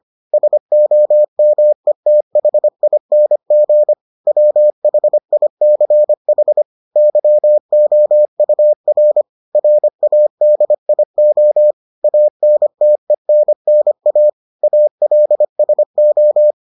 Morse code, heard as a keyed tone, about 25 wpm, 600 Hz. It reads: SOMETHING WHICH YOUR RADIO ANTENNA ALSO